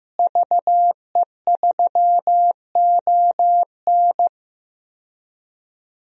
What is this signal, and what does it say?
Morse code, heard as a keyed tone, about 15 wpm, 700 Hz. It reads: VE3ON
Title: DXCC-100 Call Signs - 5 Characters - Repeated 2x - 15wpm